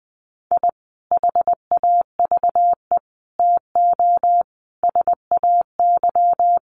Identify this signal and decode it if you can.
Morse code, heard as a keyed tone, about 20 wpm, 700 Hz. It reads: I HAVE TO SAY
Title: Sets of 5 Words Encore 20wpm